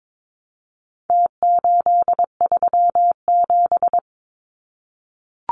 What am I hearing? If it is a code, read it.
Morse code, heard as a keyed tone, about 22 wpm, 700 Hz. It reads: T837
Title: Numbers - Contest Style - 4-Digit 22wpm